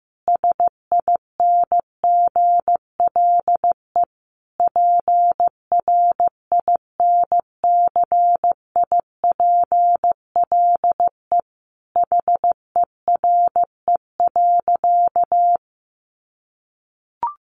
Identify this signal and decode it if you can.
Morse code, heard as a keyed tone, about 15 wpm, 700 Hz. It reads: SINGLE PRINCIPLE HERE.
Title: Sentences from Top 1400 Words 15wpm